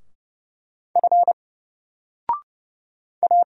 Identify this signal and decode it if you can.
Morse code, heard as a keyed tone, about 30 wpm, 700 Hz. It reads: F A
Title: Single Letter-Number 30wpm